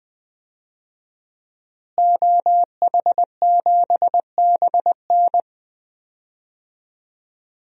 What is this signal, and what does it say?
Morse code, heard as a keyed tone, about 20 wpm, 700 Hz. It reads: OH7BN
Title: DXCC-100 Call Signs - 5 Characters - Repeated 2x - 20wpm